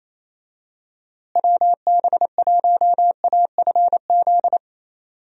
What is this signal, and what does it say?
Morse code, heard as a keyed tone, about 28 wpm, 700 Hz. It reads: WB1AFZ
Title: USA Call Signs - 6 Characters - Repeated 3x - 28wpm